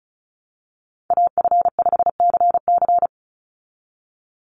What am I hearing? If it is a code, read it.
Morse code, heard as a keyed tone, about 34 wpm, 700 Hz. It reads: AF5CC
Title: Call Signs Repeated 3x - 35wpm